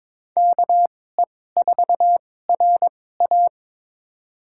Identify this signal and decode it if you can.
Morse code, heard as a keyed tone, about 21 wpm, spaced as for 16 wpm, 700 Hz. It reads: KE4RA
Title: USA Call Signs - 2x Character Spacing - 22wpm